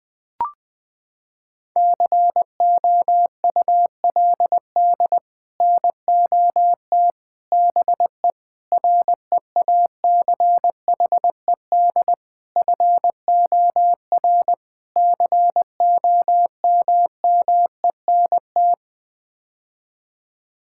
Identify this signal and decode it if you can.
Morse code, heard as a keyed tone, about 20 wpm, 700 Hz. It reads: COULD NOT BE REACHED FOR COMMENT
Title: Sets of 6 Words - 20wpm